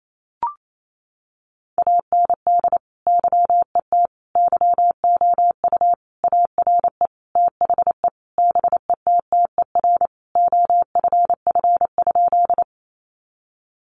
Morse code, heard as a keyed tone, about 28 wpm, 700 Hz. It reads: AND YET YOU ARE THE BETTER OFF?
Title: Sentences from Top 500 Words 28wpm